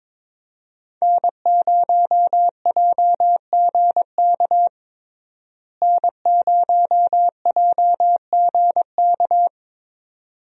Morse code, heard as a keyed tone, about 21 wpm, 700 Hz. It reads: N0JGK N0JGK
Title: USA Call Signs - 5 Characters - Repeated 2x - 22wpm